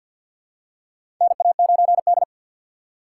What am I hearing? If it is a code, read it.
Morse code, heard as a keyed tone, about 51 wpm, 700 Hz. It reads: NA9D
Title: USA Call Signs - 4 Characters - 3x Character Spacing - 50wpm